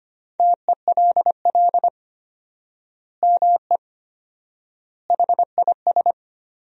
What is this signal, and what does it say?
Morse code, heard as a keyed tone, about 25 wpm, 700 Hz. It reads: TELL ME HIS
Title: 4x Word-Spacing - Sentences from Top 200 Words - 25wpm